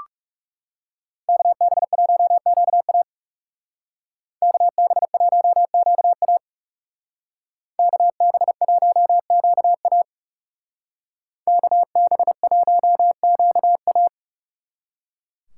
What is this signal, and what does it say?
Morse code, heard as a keyed tone, about 36 wpm, 700 Hz. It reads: KB1QA KB1QA KB1QA KB1QA
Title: Speed Racing - USA 2x2 Call Signs 30wpm